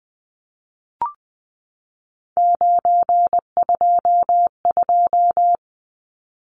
Morse code, heard as a keyed tone, about 20 wpm, 700 Hz. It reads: 922
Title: Numbers - 3-Digit 20wpm